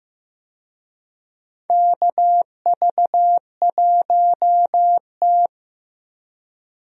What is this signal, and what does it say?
Morse code, heard as a keyed tone, about 15 wpm, 700 Hz. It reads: KV1T
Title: USA Call Signs - 4 Characters - Repeated 3x - 15wpm